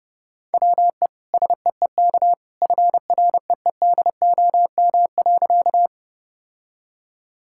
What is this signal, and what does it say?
Morse code, heard as a keyed tone, about 30 wpm, 700 Hz. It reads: WE SEEK FREEDOM.
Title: Sentences from Top 1600 Words 30wpm